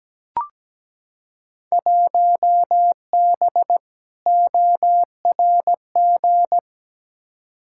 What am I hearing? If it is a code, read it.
Morse code, heard as a keyed tone, about 17 wpm, 700 Hz. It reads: 1B ORG
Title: Field Day Exchanges Encore - 17wpm